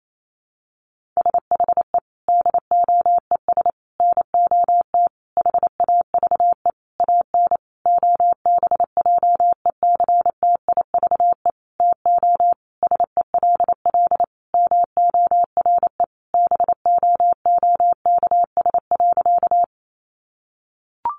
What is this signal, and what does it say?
Morse code, heard as a keyed tone, about 28 wpm, 700 Hz. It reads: SHE DOES NOT HAVE AN OBJECTIVE TO SELL MORE BOOKS.
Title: Sentences from Top 1200 Words 28wpm